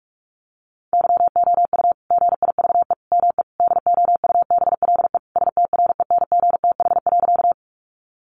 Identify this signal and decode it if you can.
Morse code, heard as a keyed tone, about 44 wpm, 700 Hz. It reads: YOU GIVE ME DOUBLE STRENGTH.
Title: Sentences from Top 1400 Words 45wpm